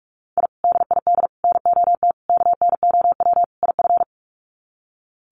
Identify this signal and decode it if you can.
Morse code, heard as a keyed tone, about 44 wpm, 700 Hz. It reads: I DID NOT KNOW IF
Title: Sets of 5 Words Encore 45wpm